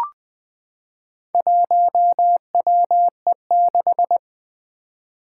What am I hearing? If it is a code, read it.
Morse code, heard as a keyed tone, about 20 wpm, 700 Hz. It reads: 1WE6
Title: Four Challenging Letter-Number Combinations 20wpm